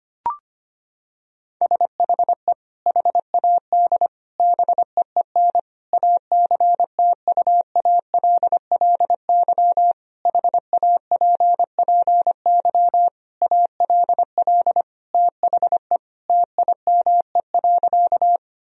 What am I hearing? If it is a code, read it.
Morse code, heard as a keyed tone, about 25 wpm, 700 Hz. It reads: SHE HAD BEEN ACTUALLY HAPPY ALL THE TIME.
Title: Sentences from Top 1000 Words - Review Part 4 - 25wpm